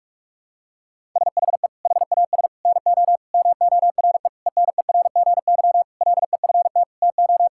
Morse code, heard as a keyed tone, about 41 wpm, 700 Hz. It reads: SHE HAS NO MORE ENERGY LEFT TO
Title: Sentences from Top 1000 Words - Review Part 6 - 45wpm